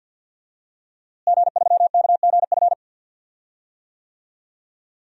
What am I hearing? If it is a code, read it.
Morse code, heard as a keyed tone, about 52 wpm, 700 Hz. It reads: G3KGF
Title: DXCC-100 Call Signs - 5 Characters - Repeated 2x - 50wpm